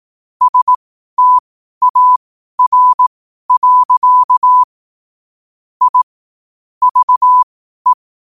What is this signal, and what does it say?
Morse code, heard as a keyed tone, about 18 wpm, spaced as for 12 wpm, 1000 Hz. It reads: STAR. I VE
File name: qotdfiles / 12wpm